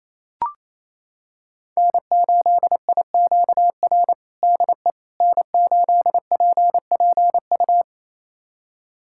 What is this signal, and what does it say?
Morse code, heard as a keyed tone, about 27 wpm, 700 Hz. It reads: N8IQR DE N8PPU
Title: USA Callsign DE Callsign - 5 Characters - 28wpm